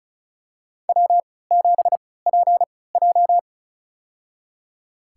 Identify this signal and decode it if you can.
Morse code, heard as a keyed tone, about 34 wpm, spaced as for 20 wpm, 700 Hz. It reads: W7PJ
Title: USA Call Signs - 4 Characters - 3x Character Spacing - 35wpm